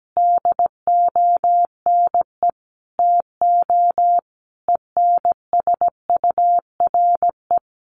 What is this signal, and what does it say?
Morse code, heard as a keyed tone, about 17 wpm, 700 Hz. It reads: DONE TO ENSURE
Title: Sentences from Top 1100 Words 17wpm